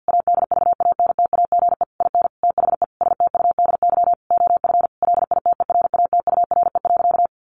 Morse code, heard as a keyed tone, about 48 wpm, 700 Hz. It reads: ADVANTAGE IN THE STUDY OF LITERATURE.
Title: Sentences from Top 900 Words 50wpm